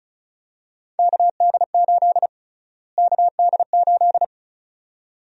Morse code, heard as a keyed tone, about 36 wpm, 700 Hz. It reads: KD8 KD8
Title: USA Call Sign Prefixes - Repeated 2x - 35wpm